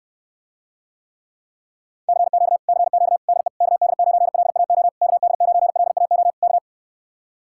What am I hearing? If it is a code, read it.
Morse code, heard as a keyed tone, about 67 wpm, 700 Hz. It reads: CQ CQ DE KD9LAO KD9LAO K